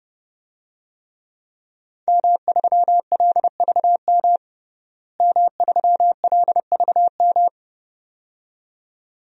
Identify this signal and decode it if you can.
Morse code, heard as a keyed tone, about 30 wpm, 700 Hz. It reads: M3LVM M3LVM
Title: DXCC-100 Call Signs - 5 Characters - Repeated 2x - 30wpm